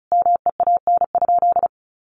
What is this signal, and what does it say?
Morse code, heard as a keyed tone, about 34 wpm, 700 Hz. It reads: MEAN?
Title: Sentences from Top 200 Words 35wpm